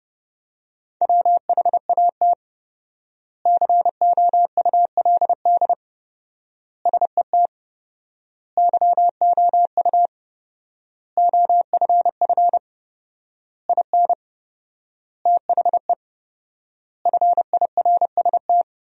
Morse code, heard as a keyed tone, about 30 wpm, 700 Hz. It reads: WHAT COULD SET YOU OFF IN THE FIRST
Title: 4x Word-Spacing - Sentences from Top 200 Words - 30wpm